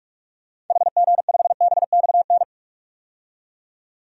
Morse code, heard as a keyed tone, about 42 wpm, 700 Hz. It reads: HG5BXN
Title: DXCC-100 Call Signs - 6 Characters - 45wpm